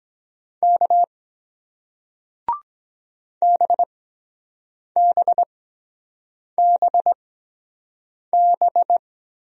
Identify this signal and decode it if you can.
Morse code, heard as a keyed tone, about 23 wpm, spaced as for 7 wpm, 700 Hz. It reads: K BBBB